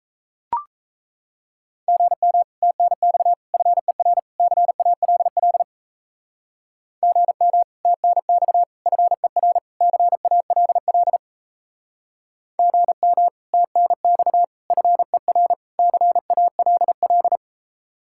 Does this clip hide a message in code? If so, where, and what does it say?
Yes: Morse code from 1.9 to 17.4 s, heard as a keyed tone, about 35 wpm, 700 Hz. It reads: GM TNX FER CALL GM TNX FER CALL GM TNX FER CALL